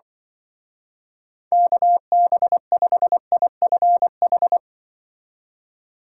Morse code, heard as a keyed tone, about 24 wpm, 700 Hz. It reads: KB5IFH
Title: Speed Racing - USA 2x3 Call Signs 20wpm